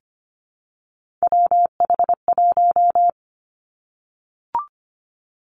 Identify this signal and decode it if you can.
Morse code, heard as a keyed tone, about 25 wpm, 700 Hz. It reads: WH1
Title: Three Challenging Letter-Number Combinations 25wpm